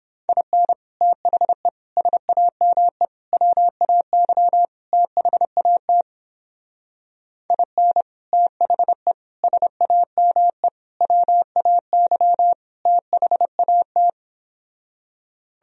Morse code, heard as a keyed tone, about 28 wpm, 700 Hz. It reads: IN THE SAME WAY THAT IN THE SAME WAY THAT